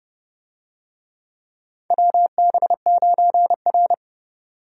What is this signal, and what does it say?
Morse code, heard as a keyed tone, about 30 wpm, 700 Hz. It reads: WB9R
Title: USA Call Signs - 4 Characters - 30wpm